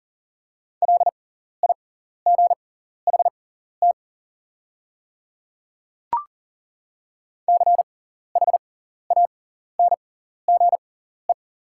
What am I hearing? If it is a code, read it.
Morse code, heard as a keyed tone, about 40 wpm, spaced as for 14 wpm, 700 Hz. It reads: LIGHT CHANGE
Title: 6x Character Spacing - Top 300 Words - 40wpm